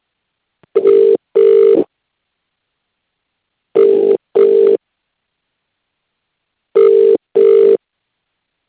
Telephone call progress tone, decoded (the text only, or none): ringback tone